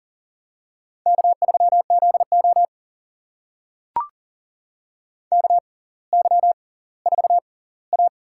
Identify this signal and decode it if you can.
Morse code, heard as a keyed tone, about 40 wpm, spaced as for 20 wpm, 700 Hz. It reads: K3ZO KY4A
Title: USA Call Signs - 4 Characters - 6x Character Spacing - 40wpm